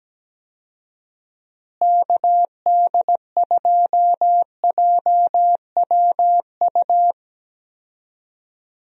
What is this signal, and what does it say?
Morse code, heard as a keyed tone, about 17 wpm, 700 Hz. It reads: KD2JWU
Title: USA Call Signs - 6 Characters - Repeated 3x - 17wpm